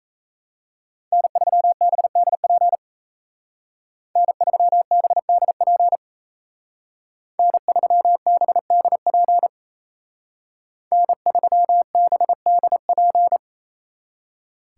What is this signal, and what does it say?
Morse code, heard as a keyed tone, about 35 wpm, 700 Hz. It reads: N3BDP N3BDP N3BDP N3BDP
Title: Speed Racing - USA 1x3 Call Signs 28wpm